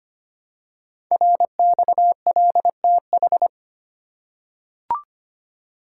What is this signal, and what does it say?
Morse code, heard as a keyed tone, about 25 wpm, 700 Hz. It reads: RXLTH
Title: Code Groups - 5 Letters - 25wpm